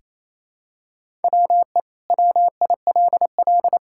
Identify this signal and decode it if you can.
Morse code, heard as a keyed tone, about 28 wpm, 700 Hz. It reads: WE WILL
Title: Sentences from Top 1000 Words - Review Part 5 - 28wpm